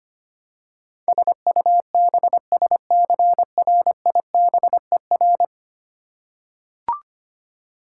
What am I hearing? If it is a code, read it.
Morse code, heard as a keyed tone, about 25 wpm, 700 Hz. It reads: SUBSCRIBER
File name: Ten Letter Words - 25wpm